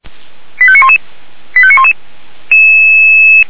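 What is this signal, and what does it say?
Five-tone selective call (ZVEI), heard as quick runs of five tones, tones of about 70 ms. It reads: 85911 85911
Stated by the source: ZVEI five-tone selective call 85381 received by SDR.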